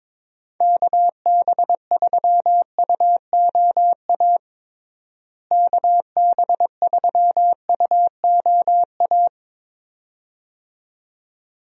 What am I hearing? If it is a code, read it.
Morse code, heard as a keyed tone, about 22 wpm, 700 Hz. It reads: KB3UOA KB3UOA
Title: USA Call Signs - 6 Characters - Repeated 2x - 22wpm